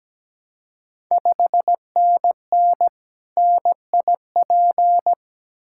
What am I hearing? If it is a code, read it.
Morse code, heard as a keyed tone, about 17 wpm, 700 Hz. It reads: 5NN NIP
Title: Ontario QSO Party Exchange 17wpm